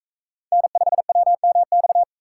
Morse code, heard as a keyed tone, about 42 wpm, 700 Hz. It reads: N5WMX